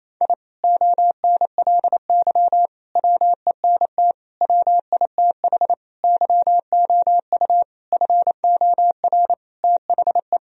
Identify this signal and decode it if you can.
Morse code, heard as a keyed tone, about 28 wpm, 700 Hz. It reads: I ONLY WENT WITH YOU FOR THE
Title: Sentences from Top 1000 Words - Review Part 1 - 28wpm